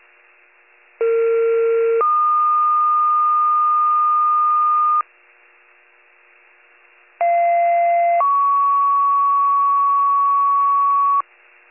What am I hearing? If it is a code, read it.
Two-tone page, heard as a one-second tone, then a three-second tone, about 460 then 1220 Hz, about 690 then 1120 Hz. It is heard from 1.0 to 11.2 s.